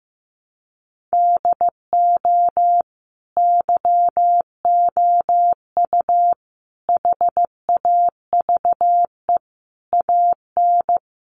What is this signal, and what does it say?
Morse code, heard as a keyed tone, about 15 wpm, 700 Hz. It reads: DO YOU HAVE AN
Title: Sentences from Top 1500 Words 15wpm